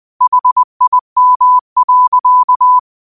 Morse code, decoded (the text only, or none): HIM.